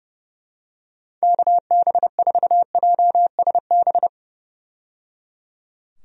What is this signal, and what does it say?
Morse code, heard as a keyed tone, about 30 wpm, 700 Hz. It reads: KB4JSB
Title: USA Call Signs - 6 Characters - 30wpm